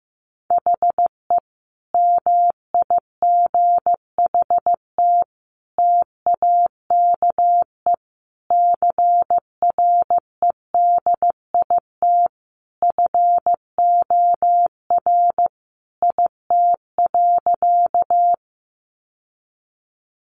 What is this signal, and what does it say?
Morse code, heard as a keyed tone, about 15 wpm, 700 Hz. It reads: HE MIGHT TAKE CREDIT FOR IT.
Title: Sentences from Top 1000 Words 15wpm